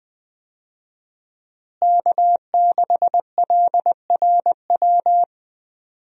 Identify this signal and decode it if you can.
Morse code, heard as a keyed tone, about 20 wpm, 700 Hz. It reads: K6LRW
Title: USA Call Signs - 5 Characters - Repeated 2x - 20wpm